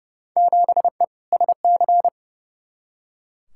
Morse code, heard as a keyed tone, about 31 wpm, 700 Hz. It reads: 7E SC
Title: Field Day Exchanges Encore - 30wpm